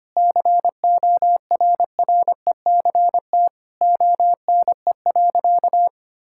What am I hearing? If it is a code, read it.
Morse code, heard as a keyed tone, about 25 wpm, 700 Hz. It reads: CORRECT ONE.
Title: Sentences from Top 1000 Words - Review Part 6 - 25wpm